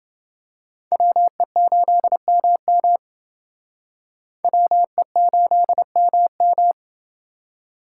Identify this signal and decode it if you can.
Morse code, heard as a keyed tone, about 28 wpm, 700 Hz. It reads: WE8MM WE8MM